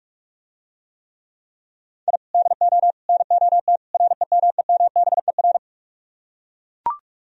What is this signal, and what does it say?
Morse code, heard as a keyed tone, about 41 wpm, 700 Hz. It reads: I DO NOT REMEMBER